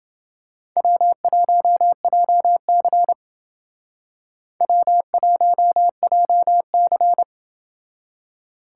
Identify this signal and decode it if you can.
Morse code, heard as a keyed tone, about 29 wpm, 700 Hz. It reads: W1JC W1JC